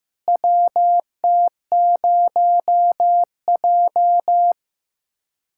Morse code, heard as a keyed tone, about 16 wpm, 700 Hz. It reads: WT0J
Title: USA Callsign DE Callsign - 4 Characters - 15wpm